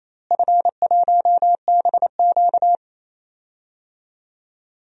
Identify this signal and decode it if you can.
Morse code, heard as a keyed tone, about 28 wpm, 700 Hz. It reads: F1BQ
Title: DXCC-100 Call Signs - 4 Characters - Repeated 2x - 28wpm